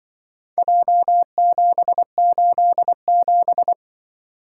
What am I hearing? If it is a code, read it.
Morse code, heard as a keyed tone, about 24 wpm, 700 Hz. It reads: J787
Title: Speed-Racing - Four Letter-Number Combinations - 20wpm